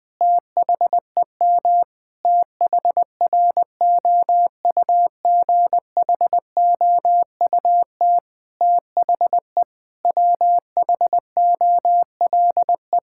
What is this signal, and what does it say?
Morse code, heard as a keyed tone, about 20 wpm, 700 Hz. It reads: THEM THROUGHOUT THE WHOLE
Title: Sentences from Top 1000 Words - Review Part 5 - 20wpm